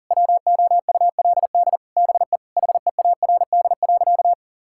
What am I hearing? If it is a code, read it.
Morse code, heard as a keyed tone, about 40 wpm, 700 Hz. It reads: WOULD BE HEARD.
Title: Sentences from Top 1600 Words 40wpm